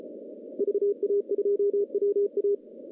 Morse code, heard as a keyed tone, about 33 wpm, 410 Hz. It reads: VA2WA
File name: contests / CW-SS-k5tr- -0100